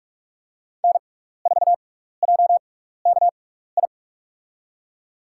Morse code, heard as a keyed tone, about 44 wpm, spaced as for 15 wpm, 700 Hz. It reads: N4JKI
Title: USA Call Signs - 5 Characters - 6x Character Spacing - 45wpm